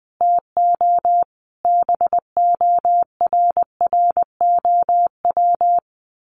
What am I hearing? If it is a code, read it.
Morse code, heard as a keyed tone, about 20 wpm, 700 Hz. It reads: TO BORROW
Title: Sentences from Top 1500 Words 20wpm